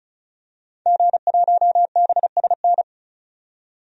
Morse code, heard as a keyed tone, about 34 wpm, 700 Hz. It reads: G1BSN
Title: DXCC-100 Call Signs - 5 Characters - 35wpm